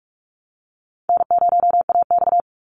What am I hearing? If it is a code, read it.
Morse code, heard as a keyed tone, about 44 wpm, 700 Hz. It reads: N0AX
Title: Easy Call Signs Repeated 3x - 45wpm